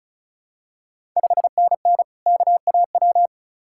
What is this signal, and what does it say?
Morse code, heard as a keyed tone, about 36 wpm, 700 Hz. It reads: 5NN KAW
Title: Ontario QSO Party Exchange 35wpm